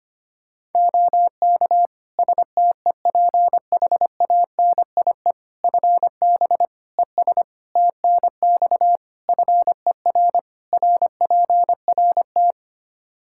Morse code, heard as a keyed tone, about 25 wpm, 700 Hz. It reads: OK STEPHANIE FB ES TNX FER RPRT
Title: Practice QSO - Exchange 2 - Info - 25wpm